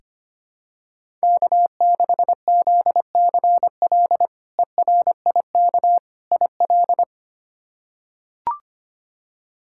Morse code, heard as a keyed tone, about 25 wpm, 700 Hz. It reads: K6ZCL ERIK IL
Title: K1USN SST QSO Practice - 25wpm